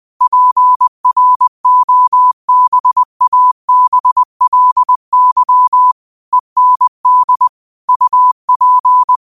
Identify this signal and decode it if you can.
Morse code, heard as a keyed tone, about 20 wpm, 1000 Hz. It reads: PROBABLY END UP
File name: qotdfiles / 20wpm